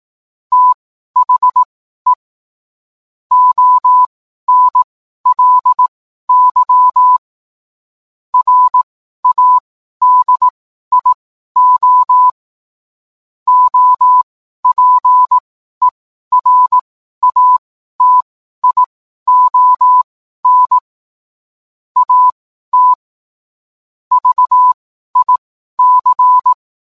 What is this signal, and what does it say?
Morse code, heard as a keyed tone, about 18 wpm, spaced as for 12 wpm, 1000 Hz. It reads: THE ONLY RADIO OPERATION AT VIC
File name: cwblog / cw mp3 directory / mem morse 12